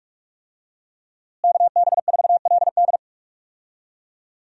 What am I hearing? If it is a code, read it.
Morse code, heard as a keyed tone, about 46 wpm, 700 Hz. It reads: KB4LD
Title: USA Call Signs - 5 Characters - Repeated 2x - 45wpm